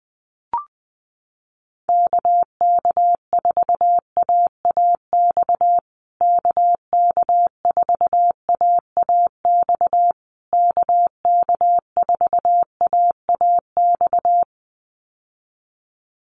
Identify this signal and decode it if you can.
Morse code, heard as a keyed tone, about 20 wpm, 700 Hz. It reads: KK4AAX KK4AAX KK4AAX
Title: USA Call Signs Repeated 3x 20wpm